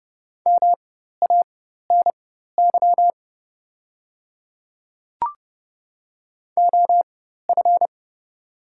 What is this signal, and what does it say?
Morse code, heard as a keyed tone, about 30 wpm, spaced as for 14 wpm, 690 Hz. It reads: MANY OF
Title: 4x Character Spacing - Top 100 Words - 30wpm